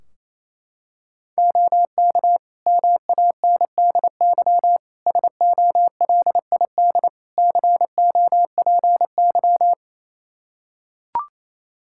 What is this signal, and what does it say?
Morse code, heard as a keyed tone, about 28 wpm, 700 Hz. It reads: OK MANDY SOLID COPY
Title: Practice QSO - Exchange 3 - More Info - 28wpm